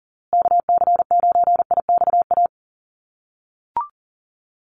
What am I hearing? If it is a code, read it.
Morse code, heard as a keyed tone, about 40 wpm, 700 Hz. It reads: KC9IXA
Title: USA Call Signs Repeated 3x 40wpm